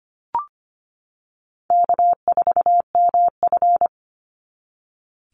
Morse code, heard as a keyed tone, about 25 wpm, 700 Hz. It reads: K4MF
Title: NA SOTA Chaser Callsigns 25wpm